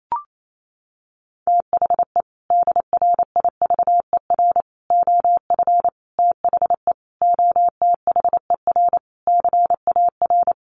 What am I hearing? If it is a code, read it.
Morse code, heard as a keyed tone, about 28 wpm, 700 Hz. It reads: THE DRIVER OF THE OTHER CAR
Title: Sentences from Top 1000 Words 28wpm